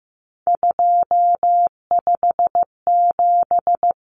Morse code, heard as a keyed tone, about 15 wpm, 700 Hz. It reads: 257
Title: Common Cut Numbers - 3-Digit 15wpm